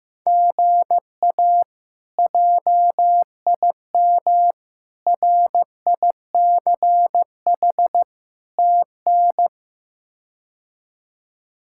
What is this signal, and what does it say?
Morse code, heard as a keyed tone, about 15 wpm, 700 Hz. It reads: GA JIM RICH TN
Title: K1USN SST QSO Practice - 15wpm